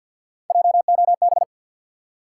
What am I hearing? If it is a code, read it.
Morse code, heard as a keyed tone, about 48 wpm, 700 Hz. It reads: JOB
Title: Three Letter Words - 50wpm